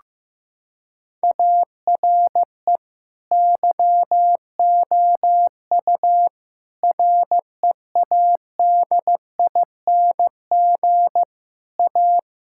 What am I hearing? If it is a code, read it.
Morse code, heard as a keyed tone, about 15 wpm, 700 Hz. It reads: ARE YOU READING A
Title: Sentences from Top 1000 Words - Review Part 6 - 15wpm